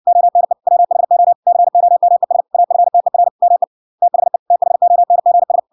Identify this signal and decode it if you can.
Morse code, heard as a keyed tone, about 56 wpm, 700 Hz. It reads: ONE WHO COMES AFTER ME THE THONGS